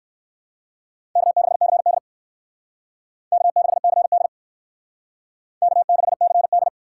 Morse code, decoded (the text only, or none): K6CD K6CD K6CD